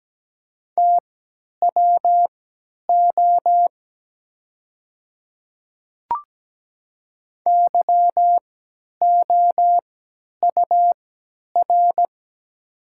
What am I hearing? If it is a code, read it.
Morse code, heard as a keyed tone, about 17 wpm, spaced as for 10 wpm, 700 Hz. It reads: TWO YOUR
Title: 3x Character Spacing - Top 200 Words - 17wpm